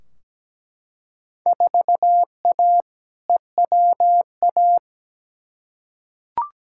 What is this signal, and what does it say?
Morse code, heard as a keyed tone, about 17 wpm, 700 Hz. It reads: 4A EWA
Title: Field Day Exchanges - 17wpm